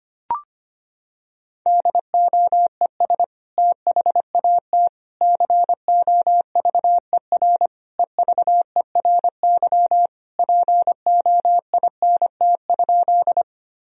Morse code, heard as a keyed tone, about 25 wpm, 700 Hz. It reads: DOES THAT COVER EVERY POINT?
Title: Sentences from Top 1000 Words - Review Part 1 - 25wpm